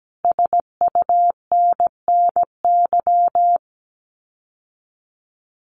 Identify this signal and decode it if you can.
Morse code, heard as a keyed tone, about 17 wpm, 700 Hz. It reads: SUNNY
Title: Common QSO Phrases Encore 17wpm